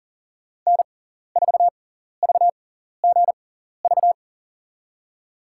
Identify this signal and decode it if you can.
Morse code, heard as a keyed tone, about 38 wpm, spaced as for 14 wpm, 700 Hz. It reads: N4VGV
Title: USA Call Signs - 5 Characters - 6x Character Spacing - 40wpm